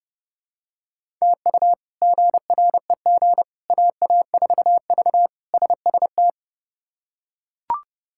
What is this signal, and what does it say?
Morse code, heard as a keyed tone, about 30 wpm, 700 Hz. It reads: TU GREG AA4V SST
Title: K1USN SST QSO Practice - 30wpm